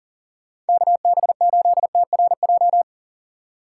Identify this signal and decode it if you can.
Morse code, heard as a keyed tone, about 48 wpm, 700 Hz. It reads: KB8TRJ
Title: Speed Racing - USA 2x3 Call Signs 40wpm